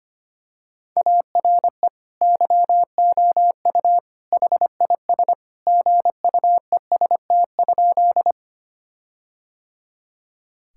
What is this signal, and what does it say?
Morse code, heard as a keyed tone, about 25 wpm, 700 Hz. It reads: ARE YOU HIS GUEST?
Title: Sentences from Top 1600 Words 25wpm